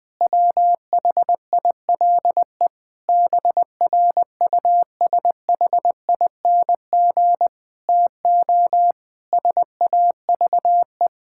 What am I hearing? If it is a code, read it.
Morse code, heard as a keyed tone, about 20 wpm, 700 Hz. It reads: WHILE BRUSHING TO SAVE
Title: Difficult Sentences with Hints - 20wpm